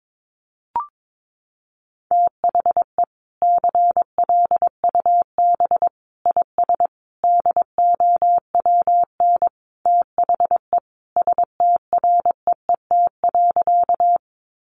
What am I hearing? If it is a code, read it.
Morse code, heard as a keyed tone, about 22 wpm, 700 Hz. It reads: THE CLUB IS DOWN THE STREET.
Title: Sentences from Top 1200 Words 22wpm